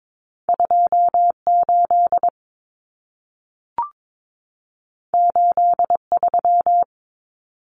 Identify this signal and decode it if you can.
Morse code, heard as a keyed tone, about 22 wpm, 700 Hz. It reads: 28 83
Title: Numbers - 2-Digit 22wpm